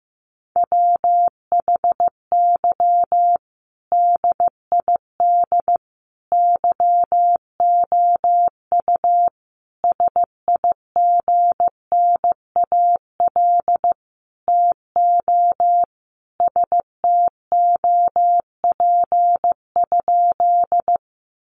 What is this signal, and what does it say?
Morse code, heard as a keyed tone, about 15 wpm, 700 Hz. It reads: WHY DID YOU SIGNAL TO STOP?
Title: Sentences from Top 1400 Words 15wpm